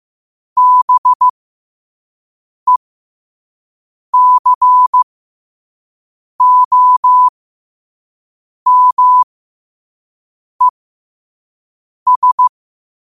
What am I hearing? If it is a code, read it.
Morse code, heard as a keyed tone, about 15 wpm, spaced as for 5 wpm, 1000 Hz. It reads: BECOMES
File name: qotdfiles / 5wpm